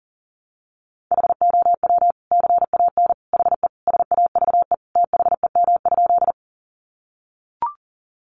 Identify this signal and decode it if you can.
Morse code, heard as a keyed tone, about 40 wpm, 700 Hz. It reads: HOW CAN HE SAVE THEM?
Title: Sentences from Top 1100 Words 40wpm